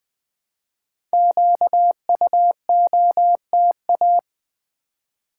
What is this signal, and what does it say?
Morse code, heard as a keyed tone, about 20 wpm, 700 Hz. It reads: QUOTA
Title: Build A Word - 1 Letter At A Time - 5-Letter Words - 20wpm